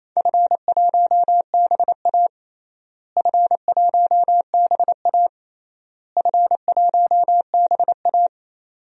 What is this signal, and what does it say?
Morse code, heard as a keyed tone, about 28 wpm, 700 Hz. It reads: F1BA F1BA F1BA
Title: DXCC-100 Call Signs - 4 Characters - Repeated 3x - 28wpm